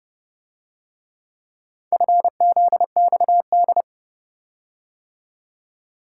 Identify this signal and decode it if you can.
Morse code, heard as a keyed tone, about 31 wpm, 700 Hz. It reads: FZXD